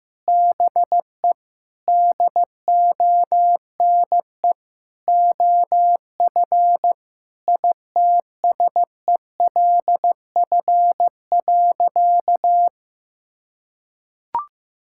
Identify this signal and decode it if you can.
Morse code, heard as a keyed tone, about 16 wpm, 700 Hz. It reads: BE DONE OF ITSELF.
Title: Sentences from Top 1000 Words - Review Part 5 - 15wpm